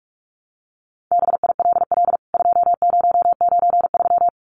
Morse code, heard as a keyed tone, about 44 wpm, 700 Hz. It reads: BILL 2093
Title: CWT Exchange 45wpm